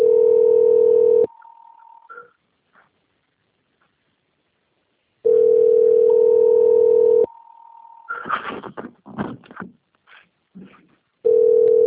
Ringback tone (North America), 440 + 480 Hz.
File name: rg-1600-unknown- - - .5926